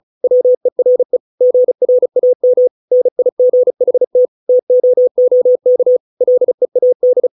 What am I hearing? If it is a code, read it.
Morse code, heard as a keyed tone, about 34 wpm, 500 Hz. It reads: WERE GRAM NIGHT TOOK LEAD